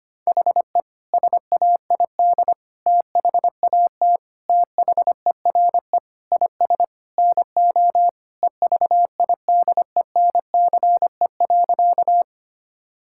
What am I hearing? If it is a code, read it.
Morse code, heard as a keyed tone, about 25 wpm, 700 Hz. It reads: HE SAID THAT THERE IS NO EVIDENCE.
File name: practice / 25wpm / Sentences with 4-Word Phrases - 25wpm